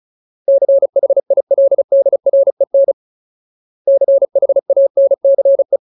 Morse code, heard as a keyed tone, about 34 wpm, 550 Hz. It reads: CHILDREN CHANCE
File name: Ultimate Word List / 35WPM Ultimate